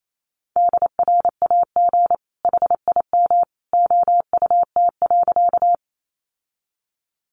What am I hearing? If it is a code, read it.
Morse code, heard as a keyed tone, about 28 wpm, 700 Hz. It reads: DRAG HIM OUT.
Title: Sentences from Top 1500 Words 28wpm